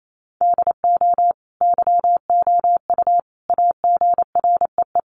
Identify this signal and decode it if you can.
Morse code, heard as a keyed tone, about 28 wpm, 700 Hz. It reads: DO YOU AGREE
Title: Sets of 5 Words Encore 28wpm